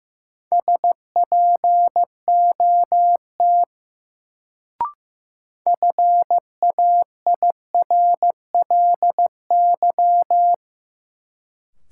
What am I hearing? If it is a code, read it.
Morse code, heard as a keyed tone, about 15 wpm, 700 Hz. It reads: SPOT FAIRLY
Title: Sentences from Top 1100 Words - New Word Drill - 15wpm